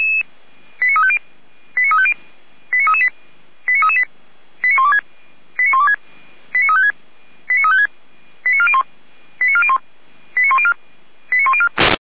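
Five-tone selective call (ZVEI), heard as quick runs of five tones, tones of about 70 ms. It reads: 89350 89350 89338 89338 89126 89126 89356 89356 89401 89401 89104 89104
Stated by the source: ZVEI five-tone selective call received on SDR.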